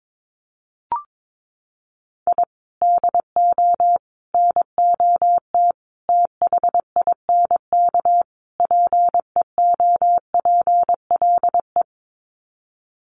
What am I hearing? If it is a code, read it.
Morse code, heard as a keyed tone, about 22 wpm, 700 Hz. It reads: I DO NOT THINK PEOPLE
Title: Sets of 5 Words Encore2 22wpm